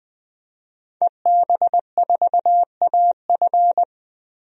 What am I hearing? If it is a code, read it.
Morse code, heard as a keyed tone, about 20 wpm, 700 Hz. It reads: EB4AF
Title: DXCC-100 Call Signs - 5 Characters - 20wpm